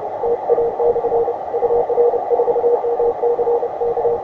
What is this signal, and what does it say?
Morse code, heard as a keyed tone, about 30 wpm, 470 Hz, with radio noise. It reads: TA/ UR4MC K
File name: attachments / ua6ypt 15m